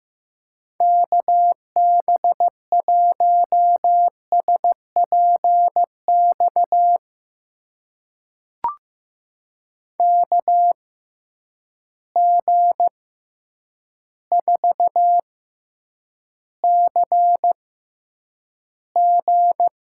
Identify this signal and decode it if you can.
Morse code, heard as a keyed tone, about 15 wpm, spaced as for 8 wpm, 700 Hz. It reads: KB1SPX KG4CG
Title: USA Call Signs - 6 Characters - 6x Character Spacing - 15wpm